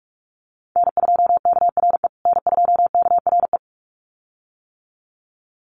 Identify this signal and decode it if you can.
Morse code, heard as a keyed tone, about 44 wpm, 700 Hz. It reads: N2KRE N2KRE
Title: USA Call Signs Repeated 2x 45wpm